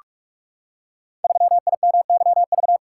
Morse code, heard as a keyed tone, about 44 wpm, 690 Hz. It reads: 3IMYV
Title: Code Groups - 5 Letter:Numbers - 45wpm